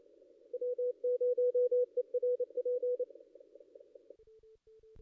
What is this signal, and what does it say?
Morse code, heard as a keyed tone, about 28 wpm, 490 Hz. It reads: W0ERP
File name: contests / ss-cw-k5tr-op-n5rz -1600